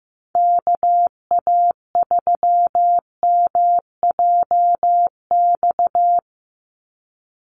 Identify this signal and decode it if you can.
Morse code, heard as a keyed tone, about 15 wpm, 700 Hz. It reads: KA3MJX
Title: USA Call Signs 15wpm